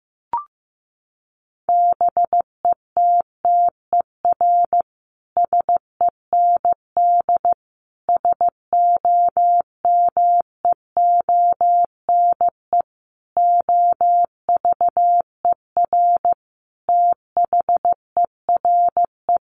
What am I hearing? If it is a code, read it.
Morse code, heard as a keyed tone, about 15 wpm, 700 Hz. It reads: BETTER SEND SOMEONE OVER THERE
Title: Sentences from Top 1500 Words 15wpm